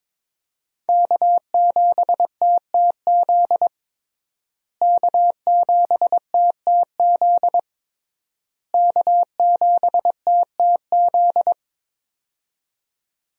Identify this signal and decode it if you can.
Morse code, heard as a keyed tone, about 22 wpm, 700 Hz. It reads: K7TTZ K7TTZ K7TTZ
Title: USA Call Signs - 5 Characters - Repeated 3x - 22wpm